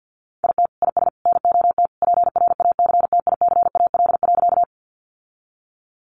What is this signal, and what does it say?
Morse code, heard as a keyed tone, about 50 wpm, 700 Hz. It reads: IT IS NOT PRACTICAL.
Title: Sentences from Top 1400 Words 50wpm